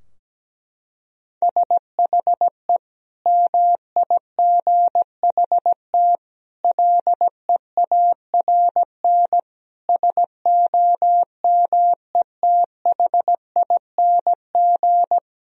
Morse code, heard as a keyed tone, about 17 wpm, 700 Hz. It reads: SHE MIGHT LEARN SOMETHING
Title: Sentences from Top 1000 Words - Review Part 4 - 17wpm